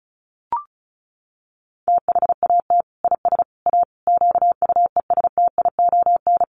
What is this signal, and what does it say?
Morse code, heard as a keyed tone, about 34 wpm, 700 Hz. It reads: THAT IS A QUESTION
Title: Sentences from Top 600 Words 35wpm